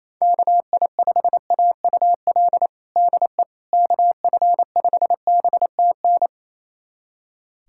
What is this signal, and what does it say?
Morse code, heard as a keyed tone, about 27 wpm, 700 Hz. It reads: KI5AUL DE KF5BTN
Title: USA Callsign DE Callsign - 6 Characters - 28wpm